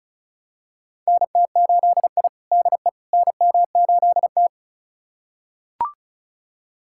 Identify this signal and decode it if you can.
Morse code, heard as a keyed tone, about 34 wpm, 700 Hz. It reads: NT8I DE NM8T
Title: USA Callsign DE Callsign - 4 Characters - 35wpm